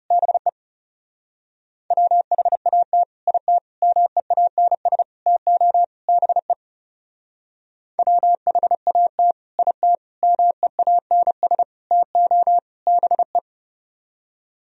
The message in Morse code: BE WHAT IT MEANS TO BE WHAT IT MEANS TO BE

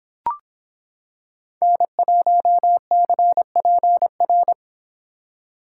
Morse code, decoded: N1CPR

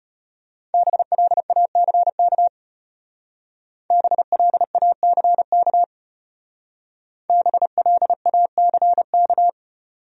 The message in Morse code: BLACK BLACK BLACK

